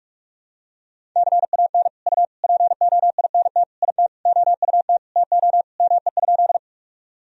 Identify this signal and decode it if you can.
Morse code, heard as a keyed tone, about 46 wpm, 700 Hz. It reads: CAN U POINT IT OUT TO ME?
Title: Sentences from Top 200 Words and Abbreviations 45wpm